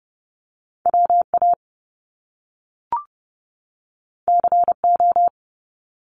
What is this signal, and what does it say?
Morse code, heard as a keyed tone, about 30 wpm, 700 Hz. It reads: WA CO
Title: US States 30wpm